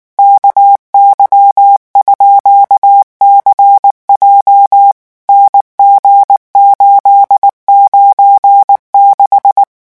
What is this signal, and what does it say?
Morse code, heard as a keyed tone, about 19 wpm, 800 Hz. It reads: KY_CJ NG896
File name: morse mp3 / revision15-vitesse 18 mots